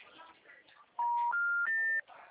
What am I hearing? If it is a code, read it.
Special information tone.